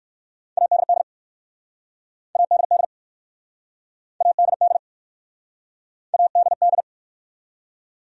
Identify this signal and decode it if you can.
Morse code, heard as a keyed tone, about 59 wpm, 700 Hz. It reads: ADD ADD ADD ADD